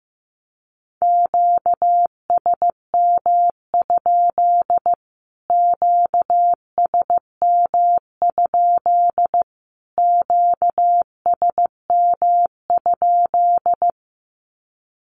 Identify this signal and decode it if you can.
Morse code, heard as a keyed tone, about 15 wpm, 700 Hz. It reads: QSM? QSM? QSM?
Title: Q-Codes Repeated 3x 15wpm